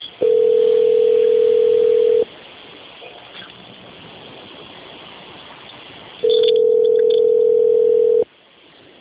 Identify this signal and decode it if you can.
Ringback tone (North America), 440 + 480 Hz.
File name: out- - - - - .19965